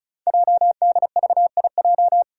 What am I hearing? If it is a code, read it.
Morse code, heard as a keyed tone, about 36 wpm, 690 Hz. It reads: JDVIJ
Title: Code Groups - 5 Letter:Numbers - 35wpm